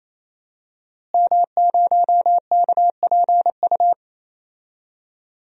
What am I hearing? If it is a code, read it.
Morse code, heard as a keyed tone, about 28 wpm, 700 Hz. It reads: M0KPU
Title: DXCC-100 Call Signs - 5 Characters - Repeated 3x - 28wpm